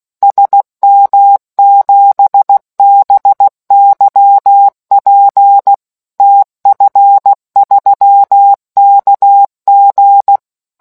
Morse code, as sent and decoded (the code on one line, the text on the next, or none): ... -- --... -... -.-- .--. / - ..-. ...-- -.- --.
SM7BYP TF3KG